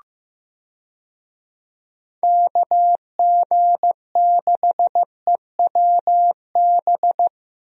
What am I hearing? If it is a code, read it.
Morse code, heard as a keyed tone, about 15 wpm, 700 Hz. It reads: KG6EWB
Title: USA Call Signs - 6 Characters - Repeated 3x - 15wpm